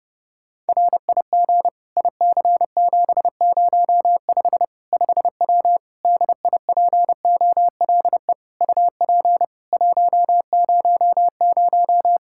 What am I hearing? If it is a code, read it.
Morse code, heard as a keyed tone, about 30 wpm, 700 Hz. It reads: RIG IC705 5W DIPOLE UP 100
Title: K1USN SST QSO Practice 30wpm